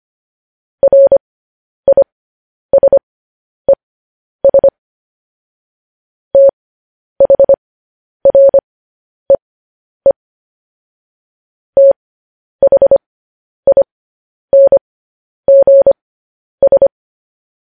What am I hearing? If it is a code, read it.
Morse code, heard as a keyed tone, about 25 wpm, spaced as for 10 wpm, 550 Hz. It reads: RISES THREE THINGS
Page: news-headlines-in-morse-code-at-10-wpm